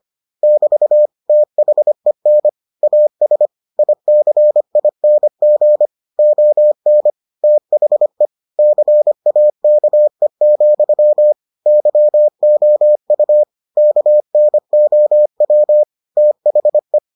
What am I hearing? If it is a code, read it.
Morse code, heard as a keyed tone, about 25 wpm, 600 Hz. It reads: <BT> THEN AS ICING ON THE CAKE, YOU KNOW THE